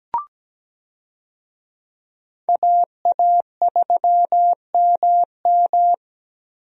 Morse code, heard as a keyed tone, about 17 wpm, 700 Hz. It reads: AA3MM
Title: USA Call Signs - 5 Characters - Repeated 3x - 17wpm